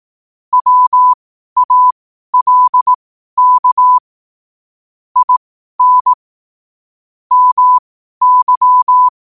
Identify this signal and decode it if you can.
Morse code, heard as a keyed tone, about 18 wpm, spaced as for 12 wpm, 1000 Hz. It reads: WALK IN MY
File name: qotdfiles / 12wpm